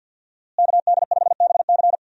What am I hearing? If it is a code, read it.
Morse code, heard as a keyed tone, about 47 wpm, 700 Hz. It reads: KD5BC
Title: USA Call Signs - 5 Characters - Repeated 2x - 50wpm